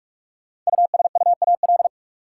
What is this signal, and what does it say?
Morse code, heard as a keyed tone, about 44 wpm, 700 Hz. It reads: USUAL
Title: Sentences from Top 400 Words - New Word Drill - 45wpm